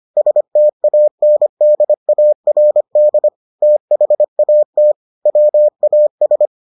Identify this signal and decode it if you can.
Morse code, heard as a keyed tone, about 25 wpm, 600 Hz. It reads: STANDARD THAT WAS